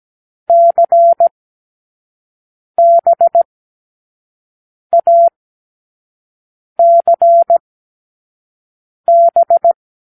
Morse code, heard as a keyed tone, about 17 wpm, spaced as for 5 wpm, 680 Hz. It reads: CBACB
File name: k7qo / Lesson6(A-D)